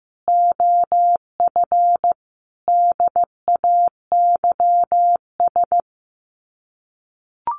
Morse code, heard as a keyed tone, about 15 wpm, 700 Hz. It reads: OF DAYS
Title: Sets of 5 Words Encore2 15wpm